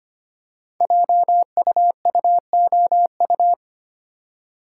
Morse code, heard as a keyed tone, about 25 wpm, 700 Hz. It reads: JUUOU
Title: Code Groups - 5 Letters - 25wpm